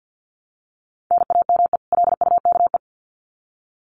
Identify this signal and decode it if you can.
Morse code, heard as a keyed tone, about 50 wpm, 700 Hz. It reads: NAME LUKE
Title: Common QSO Phrases 50wpm